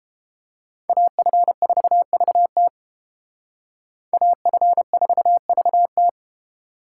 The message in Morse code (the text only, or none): AF4VT AF4VT